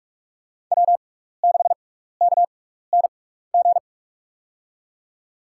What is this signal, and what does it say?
Morse code, heard as a keyed tone, about 46 wpm, spaced as for 15 wpm, 700 Hz. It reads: W6KNG
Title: USA Call Signs - 6x Character Spacing - 45wpm